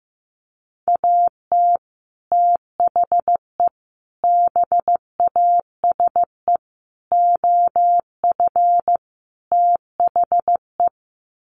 Morse code, heard as a keyed tone, about 15 wpm, 700 Hz. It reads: AT THE BASE OF THE
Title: Sets of 5 Words 15wpm